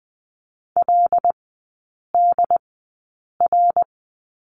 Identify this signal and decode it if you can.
Morse code, heard as a keyed tone, about 20 wpm, spaced as for 8 wpm, 700 Hz. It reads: LDR
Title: 100 Lesson 17e - Letter u - Three Characters 20wpm